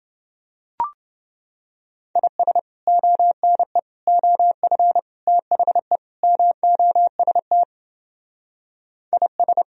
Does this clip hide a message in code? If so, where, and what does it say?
Yes: Morse code from 2.2 to 9.6 s, heard as a keyed tone, about 29 wpm, 700 Hz. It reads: IS ONE OF THE MOST IS